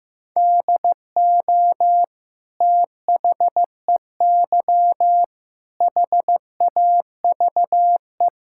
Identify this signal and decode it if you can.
Morse code, heard as a keyed tone, about 15 wpm, 700 Hz. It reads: DO THEY HAVE
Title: Sentences from Top 1600 Words 15wpm